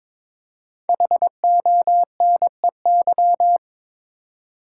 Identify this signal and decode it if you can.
Morse code, heard as a keyed tone, about 22 wpm, 690 Hz. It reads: HONEY
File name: 5 Letter Words - First 3 Letters Given - 22wpm